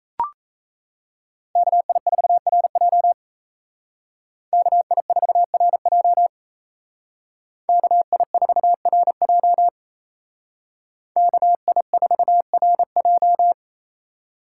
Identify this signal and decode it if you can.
Morse code, heard as a keyed tone, about 36 wpm, 700 Hz. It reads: KI4RJ KI4RJ KI4RJ KI4RJ